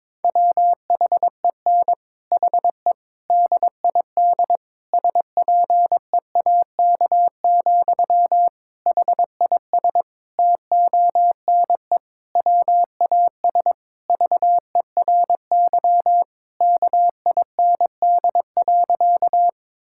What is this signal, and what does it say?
Morse code, heard as a keyed tone, about 22 wpm, 700 Hz. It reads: WHEN HE DID SPEAK, HIS TONE WAS VERY KIND.
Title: Sentences from Top 1600 Words 22wpm